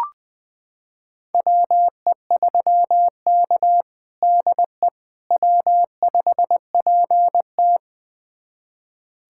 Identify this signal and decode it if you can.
Morse code, heard as a keyed tone, about 20 wpm, 700 Hz. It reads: WE3K DE W5PT
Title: USA Callsign DE Callsign - 4 Characters - 20wpm